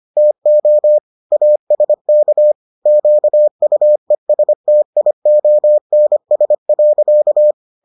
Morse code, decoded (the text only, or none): TO ASK QUESTIONS.